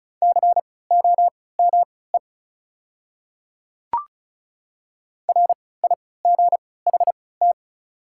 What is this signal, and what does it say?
Morse code, heard as a keyed tone, about 36 wpm, spaced as for 20 wpm, 700 Hz. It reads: COME RIGHT
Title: 3x Character Spacing - Top 200 Words - 35wpm